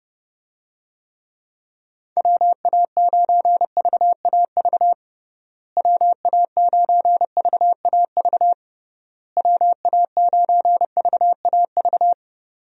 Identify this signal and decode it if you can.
Morse code, heard as a keyed tone, about 30 wpm, 700 Hz. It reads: WA9VAV WA9VAV WA9VAV
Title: USA Call Signs - 6 Characters - Repeated 3x - 30wpm